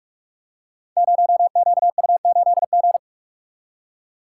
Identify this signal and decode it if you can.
Morse code, heard as a keyed tone, about 44 wpm, 700 Hz. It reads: 0QU8G
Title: Code Groups - 5 Letter:Numbers - 45wpm